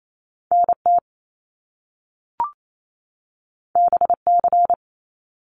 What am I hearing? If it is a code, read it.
Morse code, heard as a keyed tone, about 28 wpm, 700 Hz. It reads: NT BC
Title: Canadian Provinces and Territories 28wpm